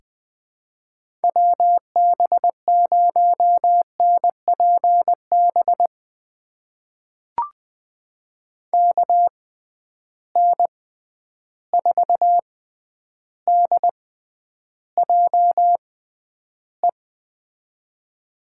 Morse code, heard as a keyed tone, about 21 wpm, spaced as for 10 wpm, 700 Hz. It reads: WB0NPB KN4DJE
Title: USA Call Signs - 6 Characters - 6x Character Spacing - 20wpm